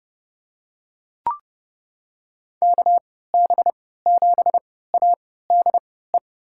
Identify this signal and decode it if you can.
Morse code, heard as a keyed tone, about 30 wpm, spaced as for 17 wpm, 700 Hz. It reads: KB7ADE
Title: USA Call Signs - 6 Characters - 3x Character Spacing - 30wpm